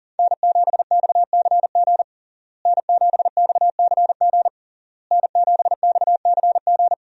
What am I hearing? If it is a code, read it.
Morse code, heard as a keyed tone, about 37 wpm, 700 Hz. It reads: N7XCG N7XCG N7XCG
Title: USA Call Signs - 5 Characters - Repeated 3x - 40wpm